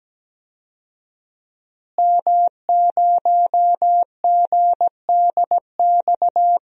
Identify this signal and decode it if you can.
Morse code, heard as a keyed tone, about 17 wpm, 700 Hz. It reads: M0GDX
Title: DXCC-100 Call Signs - 5 Characters - Repeated 2x - 17wpm